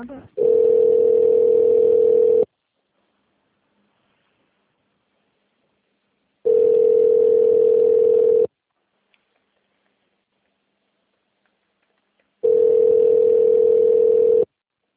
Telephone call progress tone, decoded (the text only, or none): ringback tone